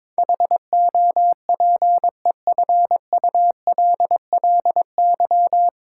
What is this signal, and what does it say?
Morse code, heard as a keyed tone, about 22 wpm, 700 Hz. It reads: HOPEFULLY
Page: Morse Code Ninja practice 22wpm